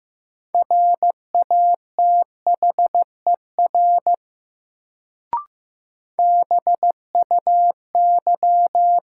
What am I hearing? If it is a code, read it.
Morse code, heard as a keyed tone, about 15 wpm, 700 Hz. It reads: RATHER BUY
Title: Sentences from Top 700 Words - New Word Drill - 15wpm